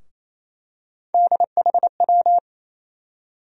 Morse code, heard as a keyed tone, about 28 wpm, 700 Hz. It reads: DHW
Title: USA Call Sign Postfixes - 3 Letters 28wpm